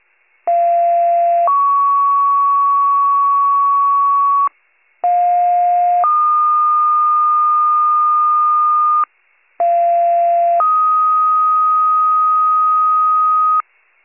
Two-tone page, heard as a one-second tone, then a three-second tone, about 690 then 1120 Hz, about 710 then 1220 Hz, about 690 then 1250 Hz. It is heard from 0.5 to 13.6 s.